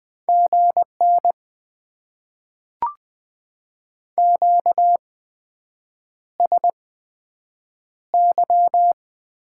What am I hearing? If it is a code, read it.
Morse code, heard as a keyed tone, about 20 wpm, spaced as for 7 wpm, 700 Hz. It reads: GN QSY
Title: Top 90 QSO Elements - 8x Character Spacing - 20wpm